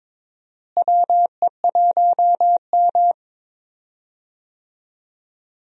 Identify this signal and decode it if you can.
Morse code, heard as a keyed tone, about 22 wpm, 700 Hz. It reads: WE1M
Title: USA Call Signs - 4 Characters - Repeated 3x - 22wpm